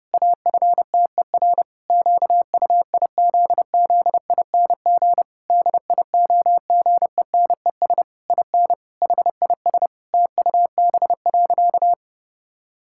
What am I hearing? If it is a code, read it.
Morse code, heard as a keyed tone, about 30 wpm, 700 Hz. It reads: AFTER QUIZZING DIOGENES IN HIS TUB.
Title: Pangram Challenge v2 - 30wpm